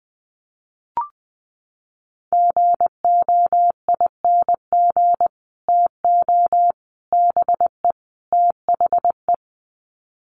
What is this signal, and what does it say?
Morse code, heard as a keyed tone, about 20 wpm, 700 Hz. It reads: GOING TO BE THE
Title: Sets of 4 Words 20wpm